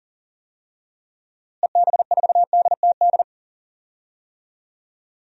Morse code, heard as a keyed tone, about 37 wpm, 700 Hz. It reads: EB4DTD
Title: DXCC-100 Call Signs - 6 Characters - Repeated 3x - 40wpm